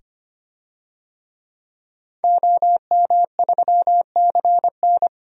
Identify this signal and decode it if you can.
Morse code, heard as a keyed tone, about 25 wpm, 700 Hz. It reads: OM3CN
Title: DXCC-100 Call Signs - 5 Characters - 25wpm